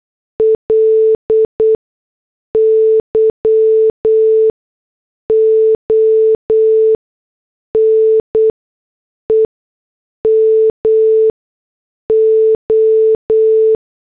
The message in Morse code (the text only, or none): LY ONE MO